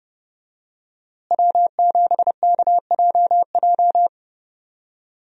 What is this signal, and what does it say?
Morse code, heard as a keyed tone, about 30 wpm, 700 Hz. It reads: W7KJJ
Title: USA Call Signs - 5 Characters - Repeated 2x - 30wpm